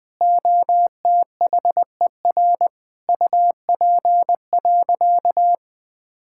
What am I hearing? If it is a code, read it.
Morse code, heard as a keyed tone, about 21 wpm, 700 Hz. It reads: OTHER UP.
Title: All-Age Appropriate Jokes - 20wpm